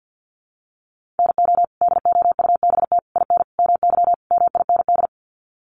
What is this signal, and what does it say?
Morse code, heard as a keyed tone, about 50 wpm, 700 Hz. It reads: NO DOUBT IN MY MIND
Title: Sets of 5 Words Encore2 50wpm